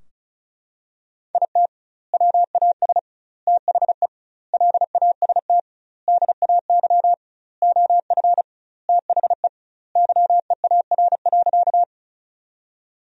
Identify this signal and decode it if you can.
Morse code, heard as a keyed tone, about 36 wpm, 700 Hz. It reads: IT WAS THE LAST DAY OF THE YEAR.
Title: 2x Word-Spacing - Sentences from Top 300 Words - 35wpm